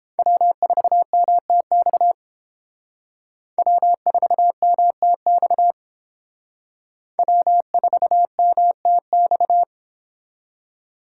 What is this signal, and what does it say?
Morse code, heard as a keyed tone, about 30 wpm, 700 Hz. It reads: W4MTX W4MTX W4MTX